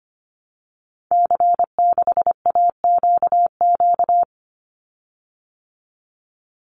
Morse code, heard as a keyed tone, about 25 wpm, 700 Hz. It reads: C6AQQ
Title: Call Signs Repeated 2x - 25wpm